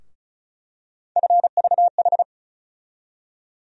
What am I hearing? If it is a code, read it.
Morse code, heard as a keyed tone, about 36 wpm, 700 Hz. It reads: FVH
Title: USA Call Sign Postfixes - 3 Letters 35wpm